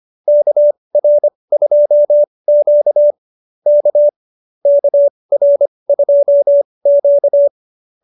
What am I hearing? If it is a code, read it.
Morse code, heard as a keyed tone, about 25 wpm, spaced as for 20 wpm, 590 Hz. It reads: KR2Q K KR2Q